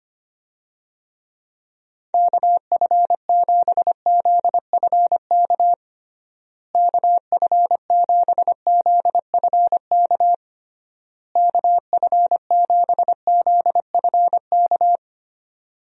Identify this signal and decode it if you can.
Morse code, heard as a keyed tone, about 25 wpm, 700 Hz. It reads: KF7ZFK KF7ZFK KF7ZFK
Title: USA Call Signs - 6 Characters - Repeated 3x - 25wpm